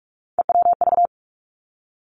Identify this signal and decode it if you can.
Morse code, heard as a keyed tone, about 46 wpm, 700 Hz. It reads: EWV
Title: Three Challenging Letter-Number Combinations 45wpm